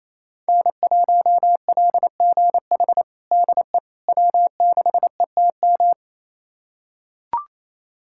Morse code, heard as a keyed tone, about 28 wpm, 700 Hz. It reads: N1LGH DE W6ETM
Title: USA Callsign DE Callsign - 5 Characters - 28wpm